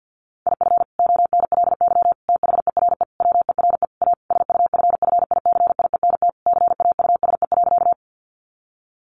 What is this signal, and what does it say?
Morse code, heard as a keyed tone, about 50 wpm, 700 Hz. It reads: IF ONLY THERE WERE A SUFFICIENT CAUSE.
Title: Sentences from Top 1400 Words 50wpm